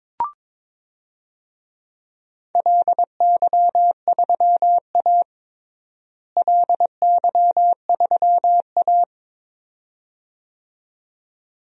Morse code, heard as a keyed tone, about 22 wpm, 700 Hz. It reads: LY3A LY3A
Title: DXCC-100 Call Signs - 4 Characters - Repeated 2x - 22wpm